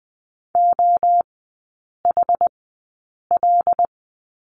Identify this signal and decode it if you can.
Morse code, heard as a keyed tone, about 20 wpm, spaced as for 8 wpm, 700 Hz. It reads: OHL
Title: 081 Lesson 14e - Number 2 - Three Characters 20wpm